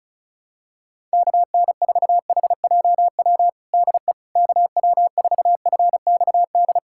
Morse code, heard as a keyed tone, about 36 wpm, 700 Hz. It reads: KN4HJW DE KW4FXD
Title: USA Callsign DE Callsign - 6 Characters - 35wpm